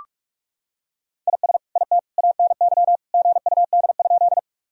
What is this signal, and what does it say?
Morse code, heard as a keyed tone, about 45 wpm, 700 Hz. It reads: IS IT ANY GUD?
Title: Sentences from Top 200 Words and Abbreviations 45wpm